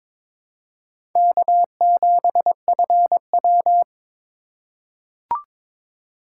Morse code, heard as a keyed tone, about 22 wpm, 700 Hz. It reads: K7FW
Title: USA Call Signs - 4 Characters - 2x Character Spacing - 22wpm